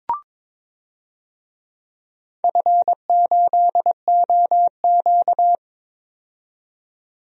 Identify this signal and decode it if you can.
Morse code, heard as a keyed tone, about 22 wpm, 700 Hz. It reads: F8OQ
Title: DXCC-100 Call Signs - 4 Characters - 22wpm